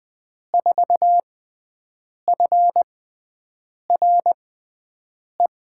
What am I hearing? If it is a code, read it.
Morse code, heard as a keyed tone, about 20 wpm, spaced as for 7 wpm, 700 Hz. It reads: 4FRE
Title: USA Call Signs - 6 Characters - 6x Character Spacing - 20wpm